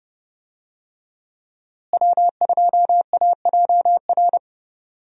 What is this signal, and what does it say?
Morse code, heard as a keyed tone, about 30 wpm, 700 Hz. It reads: W2AJR
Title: USA Call Signs - 5 Characters - Repeated 3x - 30wpm